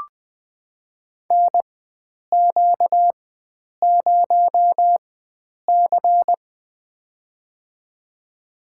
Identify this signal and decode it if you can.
Morse code, heard as a keyed tone, about 21 wpm, spaced as for 9 wpm, 700 Hz. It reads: NQ0C
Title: USA Call Signs - 4 Characters - 4x Character Spacing - 20wpm